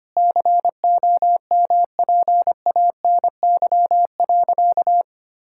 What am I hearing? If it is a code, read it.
Morse code, heard as a keyed tone, about 25 wpm, 700 Hz. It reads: COMPANY.
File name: practice / 25wpm / Sentences with 2-Word Phrases - 25wpm